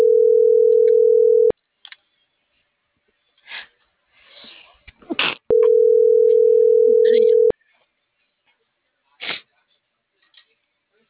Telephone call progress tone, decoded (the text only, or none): ringback tone